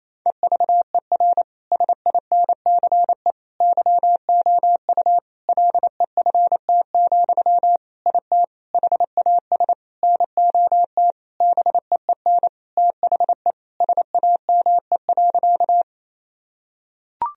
Morse code, heard as a keyed tone, about 28 wpm, 700 Hz. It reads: EVER SINCE YOU LEFT, IT HAS NOT BEEN THE SAME.
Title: Sentences from Top 1000 Words - Review Part 1 - 28wpm